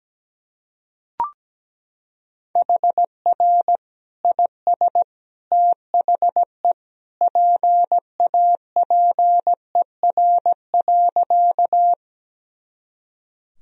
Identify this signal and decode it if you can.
Morse code, heard as a keyed tone, about 17 wpm, 700 Hz. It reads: HR IS THE PAPER.
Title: Sentences from Top 300 Words and Abbreviations 17wpm